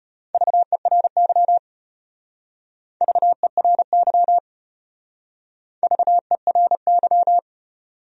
Morse code, heard as a keyed tone, about 33 wpm, 700 Hz. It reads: VERY VERY VERY